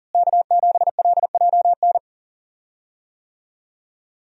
Morse code, heard as a keyed tone, about 40 wpm, 700 Hz. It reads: K7LJN
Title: USA Call Signs - 5 Characters - Repeated 2x - 40wpm